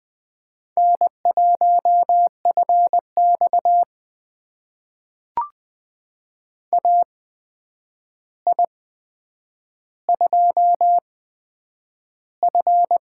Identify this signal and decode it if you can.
Morse code, heard as a keyed tone, about 20 wpm, spaced as for 9 wpm, 700 Hz. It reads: N1FX AI2F
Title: USA Call Signs - 4 Characters - 8x Character Spacing - 20wpm